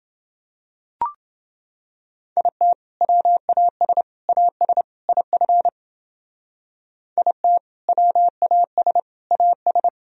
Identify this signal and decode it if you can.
Morse code, heard as a keyed tone, about 30 wpm, 700 Hz. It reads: IT WAS AS IF IT WAS AS